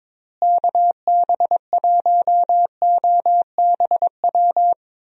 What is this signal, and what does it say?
Morse code, heard as a keyed tone, about 22 wpm, 700 Hz. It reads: KB1OBW
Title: USA Call Signs - 6 Characters - Repeated 3x - 22wpm